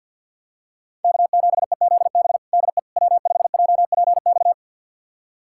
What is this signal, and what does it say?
Morse code, heard as a keyed tone, about 51 wpm, 700 Hz. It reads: K7EZB DE W5JPX
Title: USA Callsign DE Callsign - 5 Characters - 50wpm